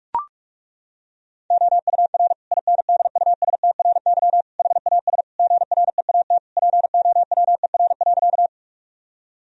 Morse code, heard as a keyed tone, about 41 wpm, 700 Hz. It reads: OUR INDUSTRY HAS GREAT POWER.
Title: Sentences from Top 1000 Words - Review Part 4 - 45wpm